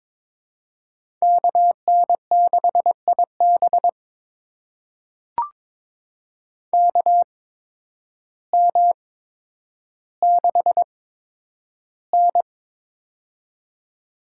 Morse code, heard as a keyed tone, about 21 wpm, spaced as for 10 wpm, 700 Hz. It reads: KN6IB KM6N
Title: USA Call Signs - 8x Character Spacing - 22wpm